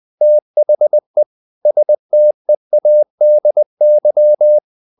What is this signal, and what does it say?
Morse code, heard as a keyed tone, about 20 wpm, 600 Hz. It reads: THE STEADY
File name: the old man and the sea / the old man and the sea part 1 20wpm